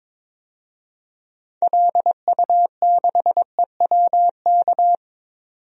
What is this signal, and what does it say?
Morse code, heard as a keyed tone, about 22 wpm, 700 Hz. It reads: LU6EWK
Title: DXCC-100 Call Signs - 6 Characters - 22wpm